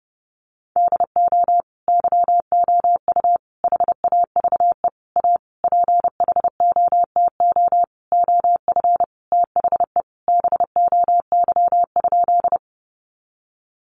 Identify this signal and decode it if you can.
Morse code, heard as a keyed tone, about 30 wpm, 700 Hz. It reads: DO YOU HAVE A PHOTO OF THE BOY?
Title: Sentences from Top 1200 Words 30wpm